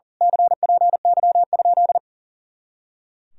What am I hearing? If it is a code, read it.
Morse code, heard as a keyed tone, about 35 wpm, 700 Hz. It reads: CPY?
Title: Speed-Racing Common QSO Phrases - 40wpm